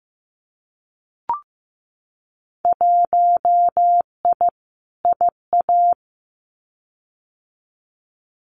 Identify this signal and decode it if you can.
Morse code, heard as a keyed tone, about 15 wpm, 700 Hz. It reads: 1I IA
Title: Winter Field Day Exchanges 15wpm